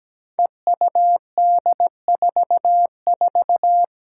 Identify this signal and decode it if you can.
Morse code, heard as a keyed tone, about 17 wpm, 700 Hz. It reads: EUD44
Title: Five Challenging Letter-Number Combinations 17wpm